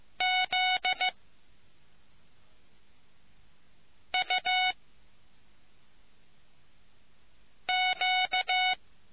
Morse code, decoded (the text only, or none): Z U Q